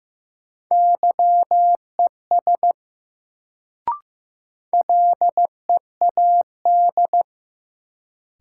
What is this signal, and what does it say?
Morse code, heard as a keyed tone, about 15 wpm, 700 Hz. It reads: YES LEAD
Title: Sentences from Top 500 Words - New Word Drill - 15wpm